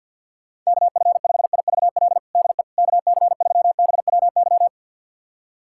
Morse code, heard as a keyed tone, about 52 wpm, 700 Hz. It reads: KF5IVL DE KC3BWY
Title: USA Callsign DE Callsign - 6 Characters - 50wpm